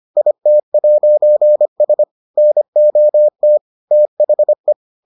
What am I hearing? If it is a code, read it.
Morse code, heard as a keyed tone, about 25 wpm, 600 Hz. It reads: IT'S NOT THE